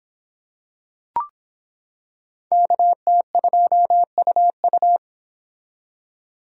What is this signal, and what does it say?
Morse code, heard as a keyed tone, about 25 wpm, 700 Hz. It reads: KT2UU